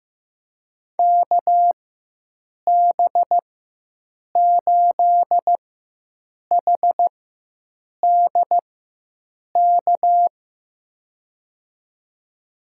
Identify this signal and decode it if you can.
Morse code, heard as a keyed tone, about 15 wpm, spaced as for 7 wpm, 700 Hz. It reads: KB8HDK
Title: USA Call Signs - 6 Characters - 4x Character Spacing - 15wpm